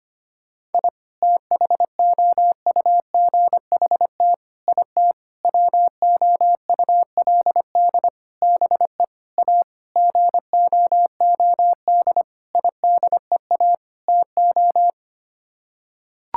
Morse code, heard as a keyed tone, about 25 wpm, 700 Hz. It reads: I THOUGHT IT WOULD BE A GOOD IDEA TO
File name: Sets of 9 Words - 25wpm